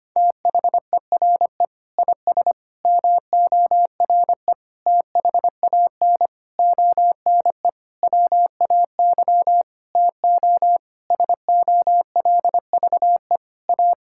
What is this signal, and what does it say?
Morse code, heard as a keyed tone, about 25 wpm, 700 Hz. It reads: THERE IS MORE THAN ONE WAY TO SOLVE A
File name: practice / 25wpm / Sentences with 9-Word Phrases - 25wpm